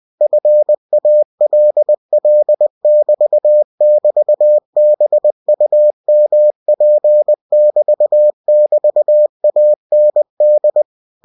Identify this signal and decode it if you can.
Morse code, heard as a keyed tone, about 20 wpm, 600 Hz. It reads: FALL<BT><BT>BUMP<BT><BT>AND